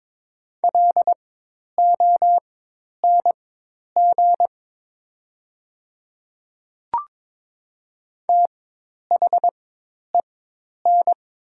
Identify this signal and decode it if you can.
Morse code, heard as a keyed tone, about 22 wpm, spaced as for 10 wpm, 700 Hz. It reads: LONG THEN
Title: 4x Character Spacing - Top 100 Words - 22wpm